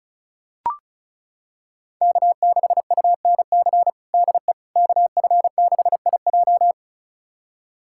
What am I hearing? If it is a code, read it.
Morse code, heard as a keyed tone, about 34 wpm, 700 Hz. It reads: K6UNC DE KF6IJ
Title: USA Callsign DE Callsign - 5 Characters - 35wpm